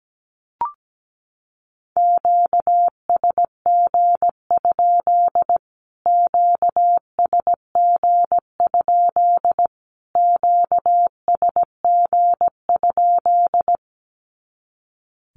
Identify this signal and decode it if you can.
Morse code, heard as a keyed tone, about 17 wpm, 700 Hz. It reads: QSG? QSG? QSG?
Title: Q-Codes Repeated 3x 17wpm